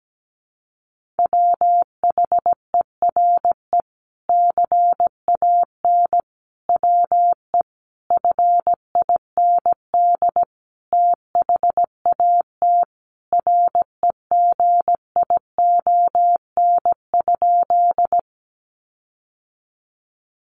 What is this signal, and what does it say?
Morse code, heard as a keyed tone, about 17 wpm, 700 Hz. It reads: WHERE CAN WE FIND THAT REGION?
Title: Sentences from Top 1500 Words 17wpm